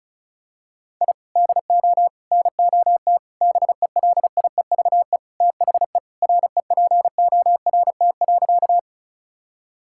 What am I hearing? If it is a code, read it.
Morse code, heard as a keyed tone, about 36 wpm, 700 Hz. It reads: I DO NOT BELIEVE THE REPORT.
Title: Sentences from Top 1000 Words - Review Part 6 - 35wpm